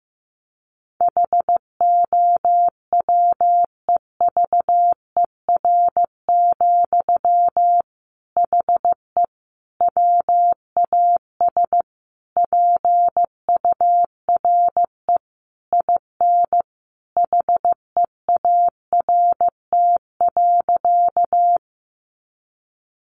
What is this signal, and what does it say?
Morse code, heard as a keyed tone, about 15 wpm, 700 Hz. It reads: HOWEVER, HE WAS PURE IN HEART.
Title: Sentences from Top 1500 Words 15wpm